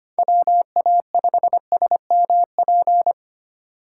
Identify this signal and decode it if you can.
Morse code, heard as a keyed tone, about 25 wpm, 700 Hz. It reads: WA5SMP
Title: USA Call Signs - 6 Characters - 4x Character Spacing - 25wpm